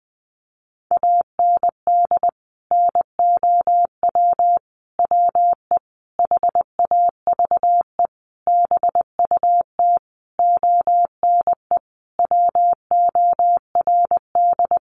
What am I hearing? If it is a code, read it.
Morse code, heard as a keyed tone, about 20 wpm, 700 Hz. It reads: AND NOW WE HAVE BUT ONE WORD
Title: Sentences from Top 1500 Words 20wpm